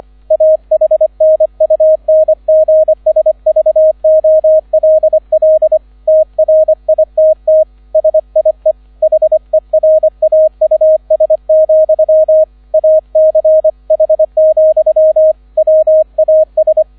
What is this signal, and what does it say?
Morse code, heard as a keyed tone, about 25 wpm, 630 Hz, with radio noise. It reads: AHNUNGSVOLL TRITT SIE HERAUS, ACH, WAS
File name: 001 Erster Streich 25wpm